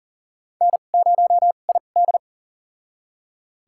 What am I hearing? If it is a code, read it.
Morse code, heard as a keyed tone, about 39 wpm, spaced as for 29 wpm, 700 Hz. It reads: N0ID
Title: USA Call Signs - 4 Characters - 2x Character Spacing - 40wpm